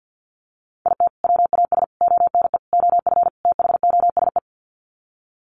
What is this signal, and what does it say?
Morse code, heard as a keyed tone, about 50 wpm, 700 Hz. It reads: IT WAS ONE OF THOSE
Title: Sets of 5 Words Encore 50wpm